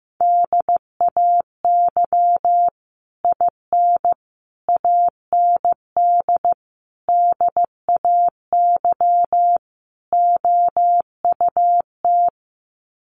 Morse code, heard as a keyed tone, about 15 wpm, 700 Hz. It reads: DAY IN AND DAY OUT
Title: Sets of 5 Words Encore2 15wpm